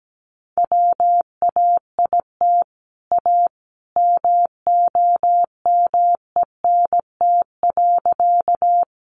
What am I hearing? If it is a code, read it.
Morse code, heard as a keyed tone, about 17 wpm, 700 Hz. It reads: WAIT A MOMENT.
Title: Sentences from Top 1000 Words 17wpm